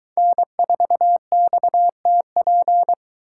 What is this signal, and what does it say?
Morse code, heard as a keyed tone, about 23 wpm, 700 Hz. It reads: N4XTP